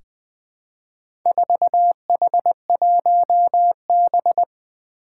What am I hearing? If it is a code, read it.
Morse code, heard as a keyed tone, about 20 wpm, 700 Hz. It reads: 4H1B
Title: Four Challenging Letter-Number Combinations 20wpm